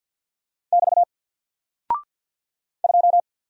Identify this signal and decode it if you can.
Morse code, heard as a keyed tone, about 54 wpm, 700 Hz. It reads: <BT> 2
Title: Single Character (Letter:Number:Punctuation:Prosign) 50wpm